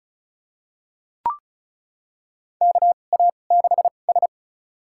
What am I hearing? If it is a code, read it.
Morse code, heard as a keyed tone, about 36 wpm, spaced as for 25 wpm, 700 Hz. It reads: KA6S
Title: USA Call Signs - 2x Character Spacing - 35wpm